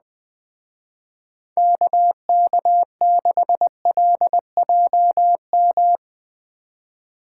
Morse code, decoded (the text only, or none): KK6LJM